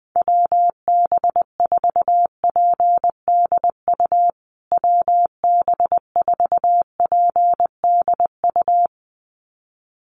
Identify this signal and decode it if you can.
Morse code, heard as a keyed tone, about 20 wpm, 700 Hz. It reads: WB4PDU WB4PDU
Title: USA Call Signs Repeated 2x 20wpm